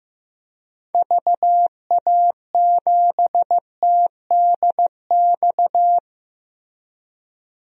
Morse code, heard as a keyed tone, about 15 wpm, 700 Hz. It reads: VA7TDX
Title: DXCC-100 Call Signs - 6 Characters - Repeated 2x - 15wpm